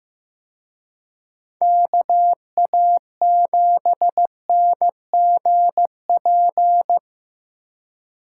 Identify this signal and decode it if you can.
Morse code, heard as a keyed tone, about 18 wpm, 700 Hz. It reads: KA7NGP
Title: Speed Racing - USA 2x3 Call Signs 15wpm